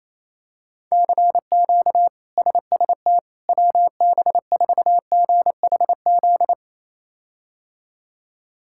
Morse code, heard as a keyed tone, about 27 wpm, 700 Hz. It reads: CQ SST WB4GHZ
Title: K1USN SST QSO Practice - 28wpm